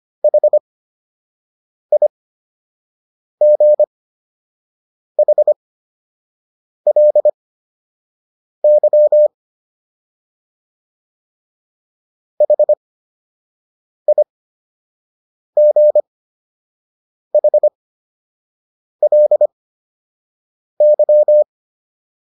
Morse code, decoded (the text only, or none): HIGHLY HIGHLY